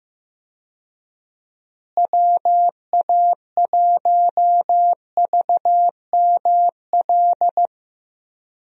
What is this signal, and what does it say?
Morse code, heard as a keyed tone, about 15 wpm, 700 Hz. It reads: WA1VML
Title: USA Call Signs - 6 Characters - Repeated 3x - 15wpm